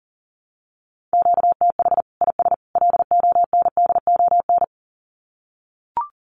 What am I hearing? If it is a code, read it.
Morse code, heard as a keyed tone, about 40 wpm, 700 Hz. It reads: QTH IS LONDON
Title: Common QSO Phrases 40wpm